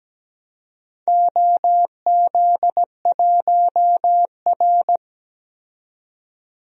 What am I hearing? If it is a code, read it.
Morse code, heard as a keyed tone, about 17 wpm, 700 Hz. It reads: OZ1R
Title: DXCC-100 Call Signs - 4 Characters - 17wpm